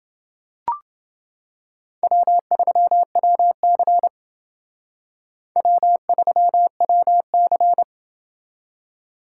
Morse code, heard as a keyed tone, about 28 wpm, 700 Hz. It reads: W3WC W3WC